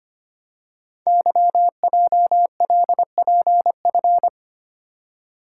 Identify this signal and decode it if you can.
Morse code, heard as a keyed tone, about 25 wpm, 700 Hz. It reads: YJLPF
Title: Code Groups - 5 Letters - 25wpm